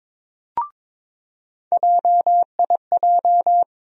Morse code, heard as a keyed tone, about 22 wpm, 700 Hz. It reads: JIJ
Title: USA Call Sign Postfixes - 3 Letters 22wpm